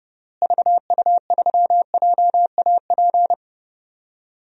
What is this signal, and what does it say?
Morse code, heard as a keyed tone, about 30 wpm, 700 Hz. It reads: VU3JAP
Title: DXCC-100 Call Signs - 6 Characters - Repeated 3x - 30wpm